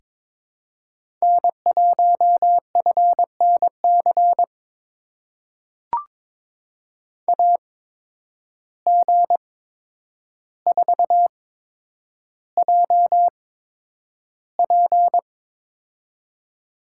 Morse code, heard as a keyed tone, about 22 wpm, spaced as for 9 wpm, 700 Hz. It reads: N1FNC AG4JP
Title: USA Call Signs - 5 Characters - 8x Character Spacing - 22wpm